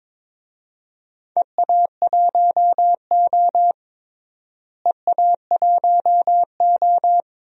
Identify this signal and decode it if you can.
Morse code, heard as a keyed tone, about 22 wpm, 700 Hz. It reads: EA1O EA1O
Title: DXCC-100 Call Signs - 4 Characters - Repeated 2x - 22wpm